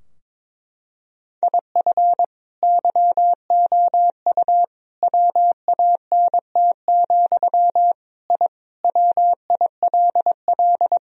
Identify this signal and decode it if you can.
Morse code, heard as a keyed tone, about 21 wpm, 700 Hz. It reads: IF YOU WANT, I WILL
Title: Sentences from Top 1000 Words - Review Part 3 - 22wpm